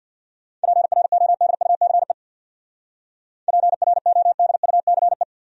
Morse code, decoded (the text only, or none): PRODUCE PRODUCE